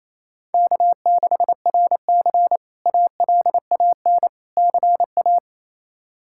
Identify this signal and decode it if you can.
Morse code, heard as a keyed tone, about 27 wpm, 700 Hz. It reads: K6RC ALAN CA
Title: K1USN SST QSO Practice - 28wpm